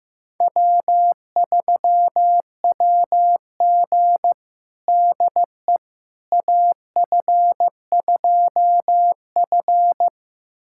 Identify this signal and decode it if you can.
Morse code, heard as a keyed tone, about 15 wpm, 700 Hz. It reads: W3WG DE AF2F
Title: USA Callsign DE Callsign - 4 Characters - 15wpm